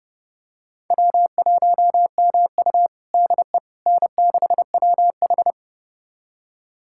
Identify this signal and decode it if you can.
Morse code, heard as a keyed tone, about 30 wpm, 700 Hz. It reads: W1MU DE N6WH
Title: USA Callsign DE Callsign - 4 Characters - 30wpm